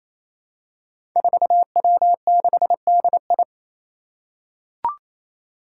Morse code, heard as a keyed tone, about 27 wpm, 700 Hz. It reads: 4W6DI
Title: Five Challenging Letter-Number Combinations 28wpm